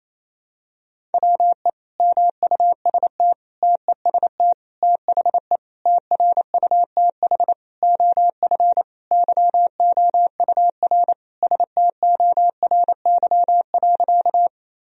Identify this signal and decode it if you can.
Morse code, heard as a keyed tone, about 28 wpm, 700 Hz. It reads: WE MUST TEST THE TRUTH OF YOUR STORY.
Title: Sentences from Top 1000 Words - Review Part 3 - 28wpm